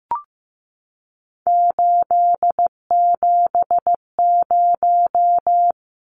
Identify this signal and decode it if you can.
Morse code, heard as a keyed tone, about 15 wpm, 700 Hz. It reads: 870
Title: Numbers - 3-Digit 15wpm